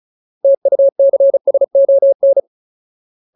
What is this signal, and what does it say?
Morse code, heard as a keyed tone, about 34 wpm, 550 Hz. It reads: TUCSON
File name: Ultimate Word List / 35WPM Ultimate